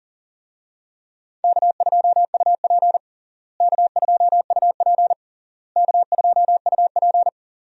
Morse code, heard as a keyed tone, about 40 wpm, 700 Hz. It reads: K2UP K2UP K2UP
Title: USA Call Signs - 4 Characters - Repeated 3x - 40wpm